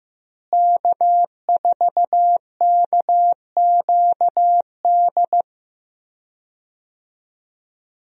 Morse code, heard as a keyed tone, about 15 wpm, 700 Hz. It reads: K4KQD
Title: USA Call Signs - 5 Characters - Repeated 3x - 15wpm